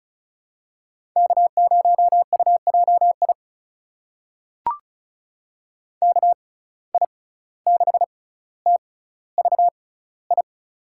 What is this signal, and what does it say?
Morse code, heard as a keyed tone, about 36 wpm, spaced as for 17 wpm, 700 Hz. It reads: K0UJI KI6TVI
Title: USA Call Signs - 6x Character Spacing - 35wpm